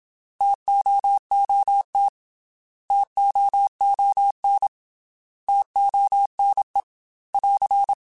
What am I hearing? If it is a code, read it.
Morse code, heard as a keyed tone, about 27 wpm, 800 Hz. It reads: TOOT TOON TONE <AR>
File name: T Prefixes / T mp3s